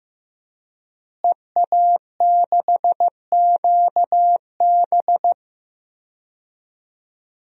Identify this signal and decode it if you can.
Morse code, heard as a keyed tone, about 15 wpm, 700 Hz. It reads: EA6QB
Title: DXCC-100 Call Signs - 5 Characters - Repeated 2x - 15wpm